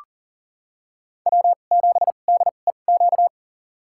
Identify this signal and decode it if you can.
Morse code, heard as a keyed tone, about 40 wpm, spaced as for 29 wpm, 700 Hz. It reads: W7DEQ
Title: USA Call Signs - 5 Characters - 2x Character Spacing - 40wpm